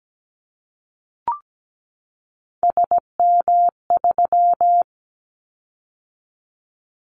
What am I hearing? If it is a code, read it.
Morse code, heard as a keyed tone, about 17 wpm, 700 Hz. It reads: SM3
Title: Callsign Prefixes for Top 25 DXCC - 17wpm